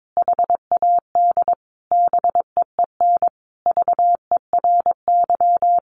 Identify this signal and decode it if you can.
Morse code, heard as a keyed tone, about 22 wpm, 700 Hz. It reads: HAD BEEN VERY
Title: Sentences from Top 1500 Words 22wpm